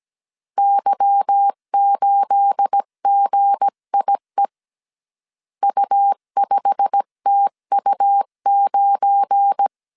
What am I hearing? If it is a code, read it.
Morse code, heard as a keyed tone, about 17 wpm, 800 Hz. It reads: Y8GIE U5TU9
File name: CW / DET / 1 9 1500C S B 16wpm 2856